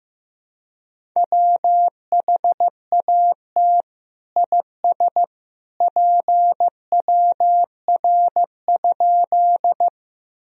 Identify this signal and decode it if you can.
Morse code, heard as a keyed tone, about 16 wpm, 700 Hz. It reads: WHAT IS PWR?
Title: Sentences from Top 500 Words and Abbreviations 15wpm